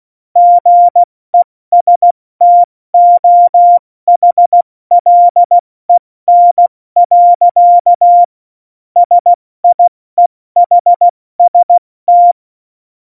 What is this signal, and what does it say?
Morse code, heard as a keyed tone, about 16 wpm, 700 Hz. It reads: GESTOHLEN. SIEHST
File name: Morsekurs / Witze / Witz 22. August 17, Tempo 70 F80